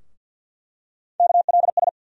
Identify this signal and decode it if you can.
Morse code, heard as a keyed tone, about 47 wpm, 700 Hz. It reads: KLS
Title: USA Call Sign Postfixes - 3 Letters 50wpm